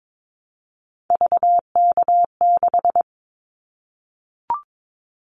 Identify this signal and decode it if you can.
Morse code, heard as a keyed tone, about 22 wpm, 700 Hz. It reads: VK6
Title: Callsign Prefixes for Top 50 DXCC - 22wpm